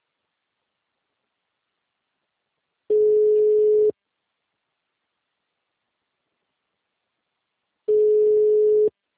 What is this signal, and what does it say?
Ringback tone (Europe, 425 Hz).